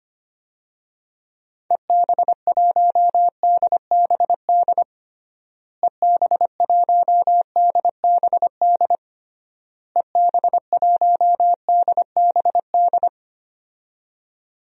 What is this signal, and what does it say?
Morse code, heard as a keyed tone, about 25 wpm, 700 Hz. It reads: EB1DBD EB1DBD EB1DBD
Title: DXCC-100 Call Signs - 6 Characters - Repeated 3x - 25wpm